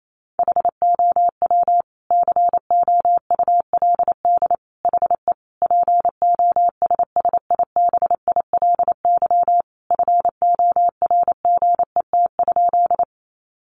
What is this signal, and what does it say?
Morse code, heard as a keyed tone, about 28 wpm, 700 Hz. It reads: HOW COULD HE POSSIBILY FORGET?
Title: Sentences from Top 1300 Words 28wpm